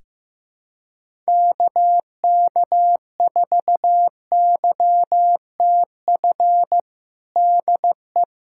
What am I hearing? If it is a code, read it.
Morse code, heard as a keyed tone, about 15 wpm, 700 Hz. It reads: KK4YTF DE
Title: USA Callsign DE Callsign - 6 Characters - 15wpm